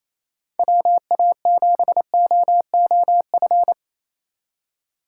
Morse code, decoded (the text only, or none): WA7OOF